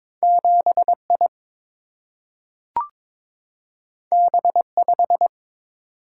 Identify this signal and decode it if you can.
Morse code, heard as a keyed tone, about 22 wpm, 700 Hz. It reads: 7I B5
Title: Single-Letter Single-Number 22wpm